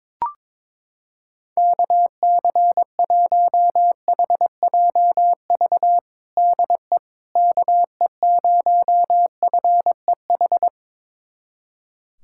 Morse code, heard as a keyed tone, about 22 wpm, 700 Hz. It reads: KC1HJV DE KE0FEH
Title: USA Callsign DE Callsign 22wpm